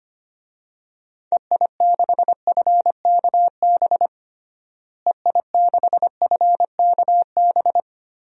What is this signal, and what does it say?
Morse code, heard as a keyed tone, about 25 wpm, 700 Hz. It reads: EI6FKB EI6FKB
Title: DXCC-100 Call Signs - 6 Characters - Repeated 2x - 25wpm